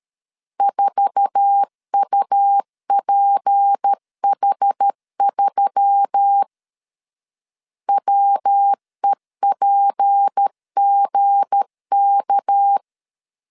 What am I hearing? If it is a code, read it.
Morse code, heard as a keyed tone, about 13 wpm, 800 Hz. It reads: 4UPH3 WEPGK
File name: LEZIONI / 8 YJ45 / 8 YJ4 12wpm 2151